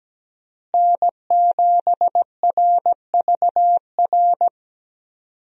Morse code, heard as a keyed tone, about 17 wpm, 700 Hz. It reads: N7RVR
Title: USA Call Signs - 5 Characters - Repeated 2x - 17wpm